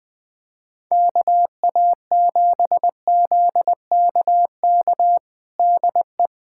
Morse code, heard as a keyed tone, about 20 wpm, 700 Hz. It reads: KA7ZKK DE
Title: USA Callsign DE Callsign - 6 Characters - 20wpm